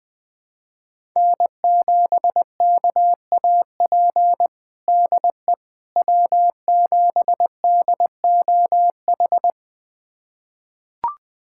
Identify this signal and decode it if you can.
Morse code, heard as a keyed tone, about 21 wpm, 700 Hz. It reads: N7KAP DE W7DOH
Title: USA Callsign DE Callsign - 5 Characters - 20wpm